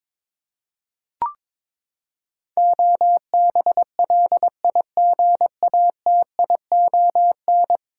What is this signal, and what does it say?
Morse code, heard as a keyed tone, about 22 wpm, 700 Hz. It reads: OBLIGATION
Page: morse-code-ninja-practice-22wpm